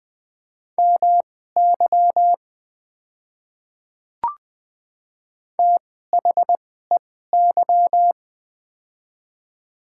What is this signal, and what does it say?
Morse code, heard as a keyed tone, about 20 wpm, spaced as for 14 wpm, 700 Hz. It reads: MY THEY
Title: 2x Character Spacing - Top 100 Words - 20wpm